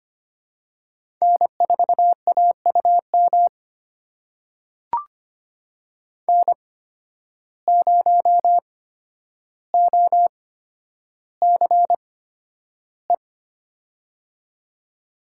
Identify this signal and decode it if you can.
Morse code, heard as a keyed tone, about 25 wpm, spaced as for 11 wpm, 700 Hz. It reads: N4AUM N0OCE
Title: USA Call Signs - 5 Characters - 8x Character Spacing - 25wpm